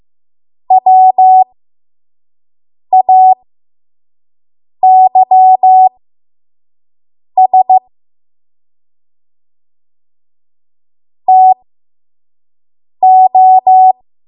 Morse code, heard as a keyed tone, about 15 wpm, spaced as for 5 wpm, 750 Hz. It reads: WAYS TO
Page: 5 WPM Code Archive